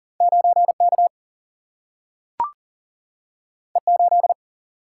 Morse code, heard as a keyed tone, about 40 wpm, 700 Hz. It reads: 9K E8
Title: Single-Letter Single-Number 40wpm